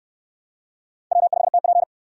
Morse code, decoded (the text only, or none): W5TP